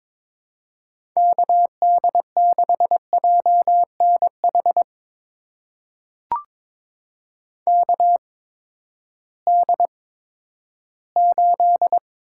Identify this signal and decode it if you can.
Morse code, heard as a keyed tone, about 22 wpm, spaced as for 13 wpm, 700 Hz. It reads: KD6JNH KD8
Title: USA Call Signs - 6 Characters - 8x Character Spacing - 22wpm